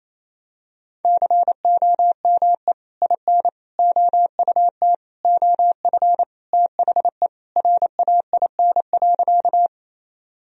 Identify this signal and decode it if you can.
Morse code, heard as a keyed tone, about 28 wpm, 700 Hz. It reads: COME IN OUT OF THE RAIN.
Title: Sentences from Top 1000 Words - Review Part 1 - 28wpm